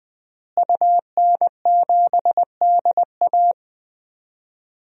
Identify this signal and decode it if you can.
Morse code, heard as a keyed tone, about 20 wpm, 700 Hz. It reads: UN7DA
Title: DXCC-100 Call Signs - 5 Characters - Repeated 2x - 20wpm